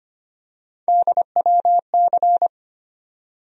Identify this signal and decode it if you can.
Morse code, heard as a keyed tone, about 25 wpm, 700 Hz. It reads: DWC
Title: USA Call Sign Postfixes - 3 Letters 25wpm